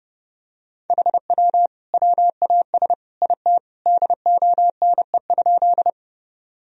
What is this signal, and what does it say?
Morse code, heard as a keyed tone, about 30 wpm, 700 Hz. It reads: HW WAS IT DONE?
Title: Sentences from Top 500 Words and Abbreviations 30wpm